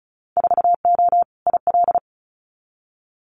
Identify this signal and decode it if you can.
Morse code, heard as a keyed tone, about 34 wpm, 700 Hz. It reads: 4O IL
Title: Winter Field Day Exchanges 35wpm